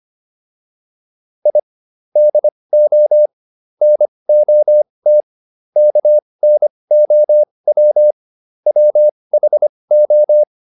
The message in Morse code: I DO NOT KNOW WHO

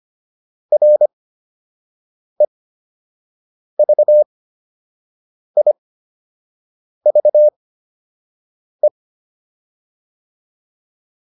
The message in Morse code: REVIVE

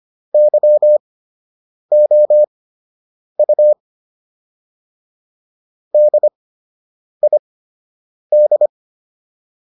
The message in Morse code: YOU DID

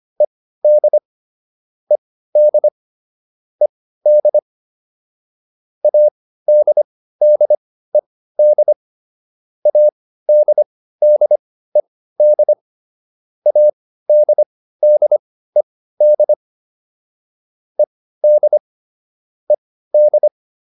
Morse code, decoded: ED ED ED ADDED ADDED ADDED ED ED

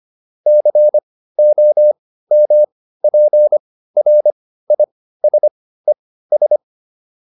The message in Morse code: COMPRISES

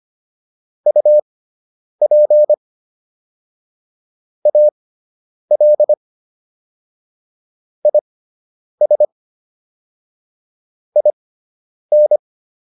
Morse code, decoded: UP AL IS IN

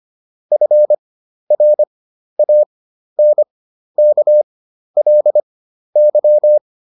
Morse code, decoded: FRANKLY

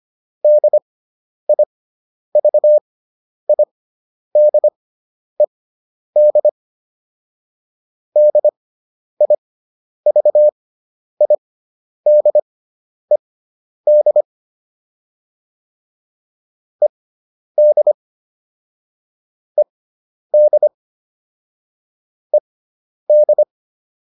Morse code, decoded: DIVIDED DIVIDED ED ED ED

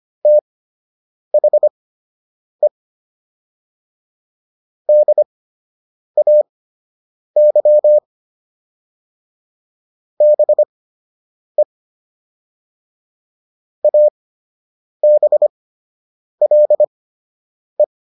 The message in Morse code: THE DAY BE ABLE

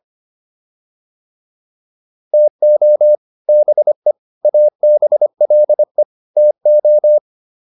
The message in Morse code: TO BE ABLE TO

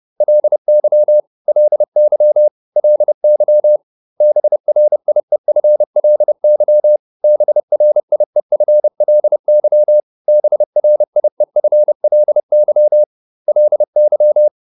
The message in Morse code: LY LY LY BRIEFLY BRIEFLY BRIEFLY LY

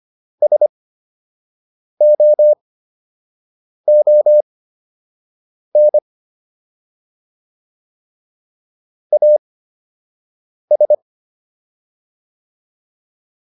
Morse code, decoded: SOON AS

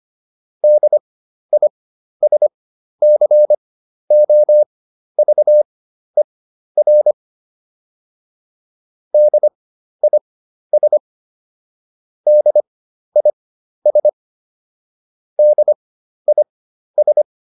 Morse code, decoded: DISCOVER DIS DIS DIS

